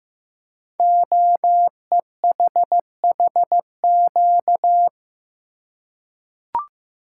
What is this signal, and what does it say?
Morse code, heard as a keyed tone, about 15 wpm, 700 Hz. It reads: OEHHQ
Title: Code Groups - 5 Letters - 15wpm